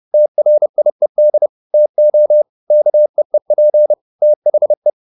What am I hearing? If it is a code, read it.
Morse code, heard as a keyed tone, about 30 wpm, 600 Hz. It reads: TRIED TO KEEP THE